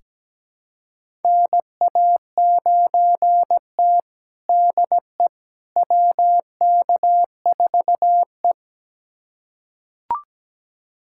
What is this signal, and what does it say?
Morse code, heard as a keyed tone, about 17 wpm, 700 Hz. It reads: NA9T DE WK4E
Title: USA Callsign DE Callsign - 4 Characters - 17wpm